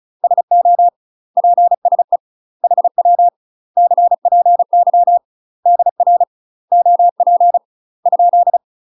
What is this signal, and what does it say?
Morse code, heard as a keyed tone, about 36 wpm, 700 Hz. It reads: SO PSE HW CPY DR OP ?
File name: UK MASTER QSOs TRAINING / 35WPM / UK MASTER QSOs 0022